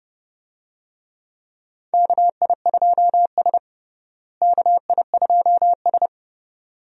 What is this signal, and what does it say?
Morse code, heard as a keyed tone, about 30 wpm, 700 Hz. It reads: KI2S KI2S
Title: USA Call Signs - 4 Characters - Repeated 2x - 30wpm